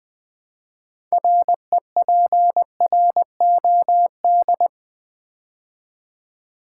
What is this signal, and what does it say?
Morse code, heard as a keyed tone, about 20 wpm, 700 Hz. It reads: REPROD
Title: Build A Word - 3 Letters At A Time - 9-Letter Words - 20wpm